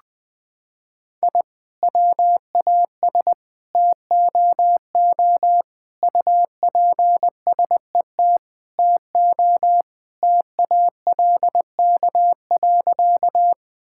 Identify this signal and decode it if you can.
Morse code, heard as a keyed tone, about 20 wpm, 700 Hz. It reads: I WAS TOO UPSET TO TALK.
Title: Sentences from Top 1000 Words - Review Part 5 - 20wpm